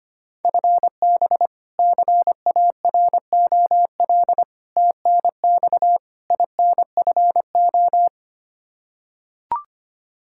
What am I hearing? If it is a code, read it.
Morse code, heard as a keyed tone, about 25 wpm, 700 Hz. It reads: FB CAROL TNX INFO
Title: Practice QSO - Exchange 3 - More Info - 25wpm